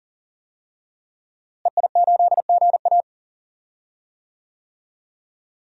Morse code, heard as a keyed tone, about 40 wpm, 700 Hz. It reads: EI8GA
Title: DXCC-100 Call Signs - 5 Characters - Repeated 3x - 40wpm